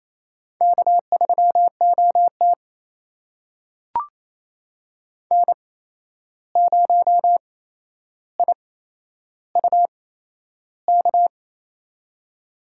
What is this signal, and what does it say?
Morse code, heard as a keyed tone, about 28 wpm, spaced as for 11 wpm, 700 Hz. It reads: K3OT N0IUK
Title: USA Call Signs - 8x Character Spacing - 28wpm